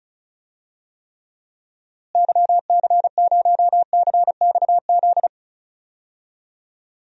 Morse code, heard as a keyed tone, about 34 wpm, 700 Hz. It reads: YC0CXZ
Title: DXCC-100 Call Signs - 6 Characters - Repeated 2x - 35wpm